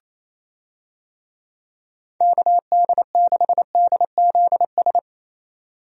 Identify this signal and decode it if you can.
Morse code, heard as a keyed tone, about 27 wpm, 700 Hz. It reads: KD6DZS
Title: USA Call Signs - 6 Characters - 28wpm